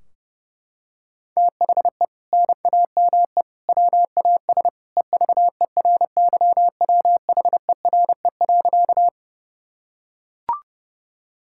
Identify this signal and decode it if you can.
Morse code, heard as a keyed tone, about 30 wpm, 700 Hz. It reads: THE NAME WAS EVERYWHERE.
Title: Sentences from Top 1600 Words 30wpm